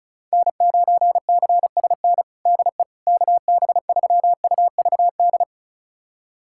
Morse code, heard as a keyed tone, about 36 wpm, 700 Hz. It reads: N9CSN DE KB3UVD
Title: USA Callsign DE Callsign 35wpm